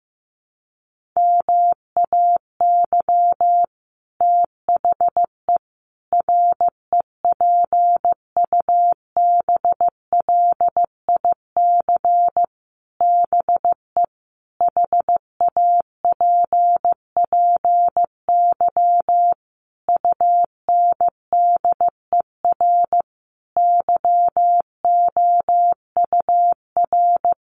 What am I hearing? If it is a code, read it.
Morse code, heard as a keyed tone, about 15 wpm, 700 Hz. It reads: MAY THE REPUBLIC BE HAPPY UNDER YOUR
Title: Sentences from Top 1500 Words 15wpm